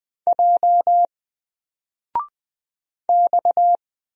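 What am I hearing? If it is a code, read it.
Morse code, heard as a keyed tone, about 20 wpm, 700 Hz. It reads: J X
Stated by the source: Single letter and number Morse drill at 20 wpm, from N4LCD.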